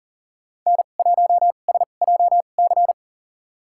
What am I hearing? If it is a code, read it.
Morse code, heard as a keyed tone, about 37 wpm, spaced as for 28 wpm, 700 Hz. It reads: N1SJC
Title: USA Call Signs - 2x Character Spacing - 40wpm